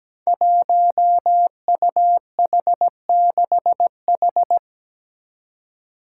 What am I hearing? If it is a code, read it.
Morse code, heard as a keyed tone, about 17 wpm, 700 Hz. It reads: 1UH6H
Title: Five Challenging Letter-Number Combinations 17wpm